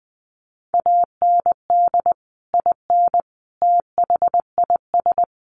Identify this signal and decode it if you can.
Morse code, heard as a keyed tone, about 20 wpm, 700 Hz. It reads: AND IN THIS
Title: Sentences from Top 1400 Words 20wpm